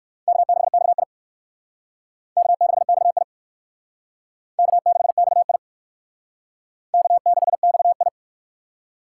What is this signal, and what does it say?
Morse code, heard as a keyed tone, about 60 wpm, 700 Hz. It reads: K6XI K6XI K6XI K6XI